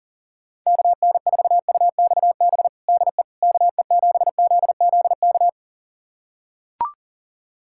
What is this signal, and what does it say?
Morse code, heard as a keyed tone, about 40 wpm, 700 Hz. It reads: KN4UXB DE KE7ZZK
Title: USA Callsign DE Callsign - 6 Characters - 40wpm